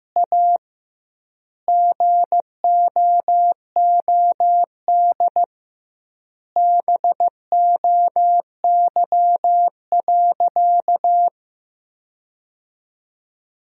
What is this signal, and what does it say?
Morse code, heard as a keyed tone, about 16 wpm, 700 Hz. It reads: A GOOD BOY.
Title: 2x Word-Spacing - Sentences from Top 200 Words - 15wpm